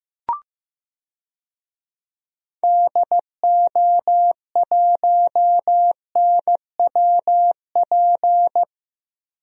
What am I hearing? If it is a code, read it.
Morse code, heard as a keyed tone, about 15 wpm, 700 Hz. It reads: DO1NWP
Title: DXCC-100 Call Signs - 6 Characters - 15wpm